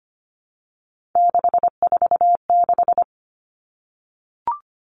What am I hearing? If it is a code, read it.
Morse code, heard as a keyed tone, about 25 wpm, 700 Hz. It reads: 646
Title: Three Challenging Letter-Number Combinations 25wpm